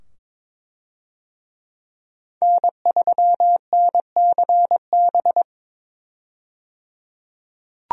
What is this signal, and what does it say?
Morse code, heard as a keyed tone, about 22 wpm, 700 Hz. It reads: N3NCB
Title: USA Call Signs - 5 Characters - 22wpm